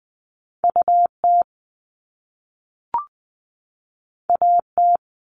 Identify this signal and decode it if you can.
Morse code, heard as a keyed tone, about 20 wpm, 700 Hz. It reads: U T A T
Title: Top Two Letter Combinations in Words 20wpm